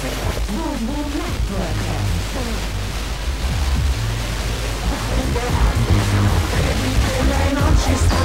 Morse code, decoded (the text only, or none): none